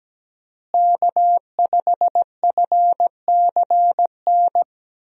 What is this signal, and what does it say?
Morse code, heard as a keyed tone, about 17 wpm, 700 Hz. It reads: K5FCN
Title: USA Call Signs - 5 Characters - Repeated 3x - 17wpm